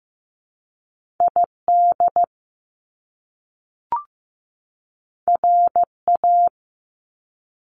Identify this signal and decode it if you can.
Morse code, heard as a keyed tone, about 15 wpm, 700 Hz. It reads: ID RA
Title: Top Two Letter Combinations in Words 15wpm